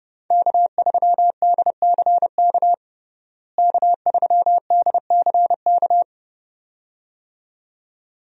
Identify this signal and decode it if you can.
Morse code, heard as a keyed tone, about 30 wpm, 700 Hz. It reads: K3DCK K3DCK
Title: USA Call Signs - 5 Characters - Repeated 2x - 30wpm